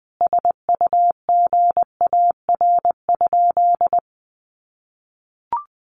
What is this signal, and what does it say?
Morse code, heard as a keyed tone, about 20 wpm, 700 Hz. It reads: SUGAR?
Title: Sentences from Top 1200 Words 20wpm